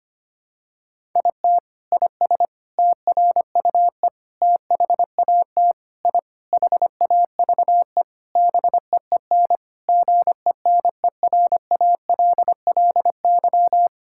Morse code, heard as a keyed tone, about 25 wpm, 700 Hz. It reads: IT IS TRUE THAT I HAVE BEEN GENERALLY
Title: Sentences from Top 1000 Words - Review Part 6 - 25wpm